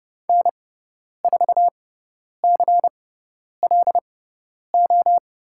Morse code, heard as a keyed tone, about 30 wpm, spaced as for 10 wpm, 700 Hz. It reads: N4CLO
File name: Lesson 16g - Letter c - Call Signs 30wpm